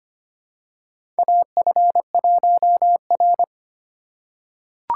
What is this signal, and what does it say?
Morse code, heard as a keyed tone, about 25 wpm, 700 Hz. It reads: AF1R
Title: USA Call Signs - 4 Characters - 2x Character Spacing - 25wpm